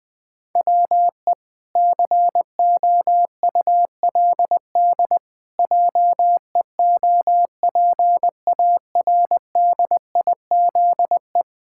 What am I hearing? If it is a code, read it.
Morse code, heard as a keyed tone, about 20 wpm, 700 Hz. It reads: WE COULD JEOPARDIZE
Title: Pangram Challenge v2 - 20wpm